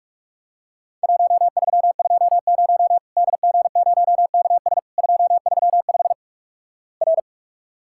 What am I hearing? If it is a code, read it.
Morse code, heard as a keyed tone, about 46 wpm, 700 Hz. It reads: 1320 DG0KS 235 R
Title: Worked All Europe DX Contest - QTC Exchange - 45wpm